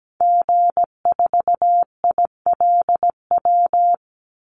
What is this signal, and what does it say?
Morse code, heard as a keyed tone, about 17 wpm, 700 Hz. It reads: G4ILW
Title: Call Signs Repeated 2x - 17wpm